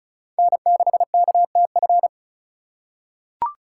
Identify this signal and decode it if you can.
Morse code, heard as a keyed tone, about 36 wpm, 700 Hz. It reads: N6KTF
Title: USA Call Signs - 8x Character Spacing - 35wpm